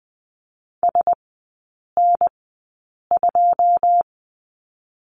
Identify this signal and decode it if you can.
Morse code, heard as a keyed tone, about 20 wpm, spaced as for 8 wpm, 700 Hz. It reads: SN2
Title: 081 Lesson 14e - Number 2 - Three Characters 20wpm